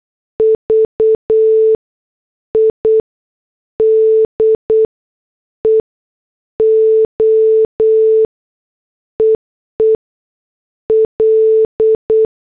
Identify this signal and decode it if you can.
Morse code, heard as a keyed tone, about 8 wpm, 440 Hz. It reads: V I D E O EE L
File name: morse / headlines